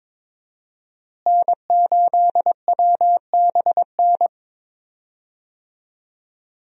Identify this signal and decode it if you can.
Morse code, heard as a keyed tone, about 22 wpm, 700 Hz. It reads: N8WBN
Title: USA Call Signs - 5 Characters - Repeated 2x - 22wpm